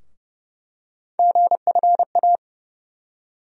Morse code, heard as a keyed tone, about 31 wpm, 700 Hz. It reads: GFA
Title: USA Call Sign Postfixes - 3 Letters 30wpm